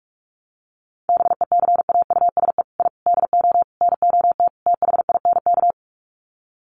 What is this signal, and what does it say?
Morse code, heard as a keyed tone, about 44 wpm, 700 Hz. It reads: BECAUSE I DO NOT THINK
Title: Sets of 5 Words Encore 45wpm